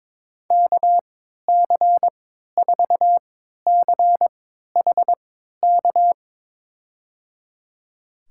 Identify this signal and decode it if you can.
Morse code, heard as a keyed tone, about 22 wpm, spaced as for 12 wpm, 700 Hz. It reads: KC4CHK
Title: USA Call Signs - 6 Characters - 3x Character Spacing - 22wpm